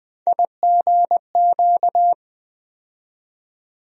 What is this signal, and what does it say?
Morse code, heard as a keyed tone, about 20 wpm, 700 Hz. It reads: IGQ
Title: USA Call Sign Postfixes - 3 Letters 20wpm